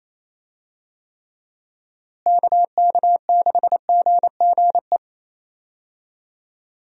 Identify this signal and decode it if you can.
Morse code, heard as a keyed tone, about 28 wpm, 700 Hz. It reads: KK6GGE
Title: USA Call Signs - 6 Characters - Repeated 2x - 28wpm